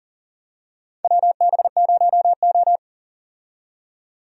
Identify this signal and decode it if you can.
Morse code, heard as a keyed tone, about 40 wpm, 700 Hz. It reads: WB0O
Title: USA Call Signs - 4 Characters - 40wpm